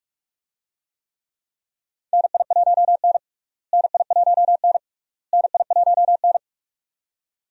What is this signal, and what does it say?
Morse code, heard as a keyed tone, about 46 wpm, 700 Hz. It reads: NI1N NI1N NI1N
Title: USA Call Signs - 4 Characters - Repeated 3x - 45wpm